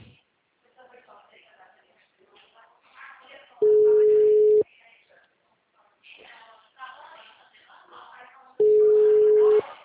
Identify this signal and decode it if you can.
Ringback tone (Europe, 425 Hz), with speech.
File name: out- -2014- - - .31735